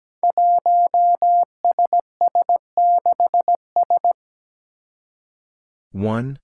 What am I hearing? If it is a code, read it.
Morse code, heard as a keyed tone, about 17 wpm, 700 Hz. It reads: 1SS6S
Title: Five Challenging Letter-Number Combinations 17wpm